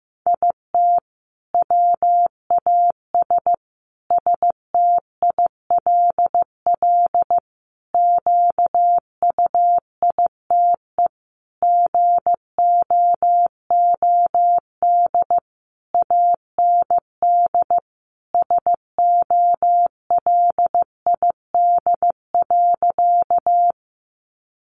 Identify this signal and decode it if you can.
Morse code, heard as a keyed tone, about 14 wpm, 700 Hz. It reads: IT WAS STILL QUITE GOOD AND SOLID.
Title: Sentences from Top 1500 Words 15wpm